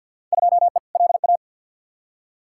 Morse code, heard as a keyed tone, about 53 wpm, 700 Hz. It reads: 2E LA
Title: Field Day Exchanges Encore - 50wpm